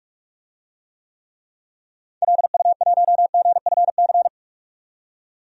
Morse code, heard as a keyed tone, about 46 wpm, 700 Hz. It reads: LU1GFC
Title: DXCC-100 Call Signs - 6 Characters - 45wpm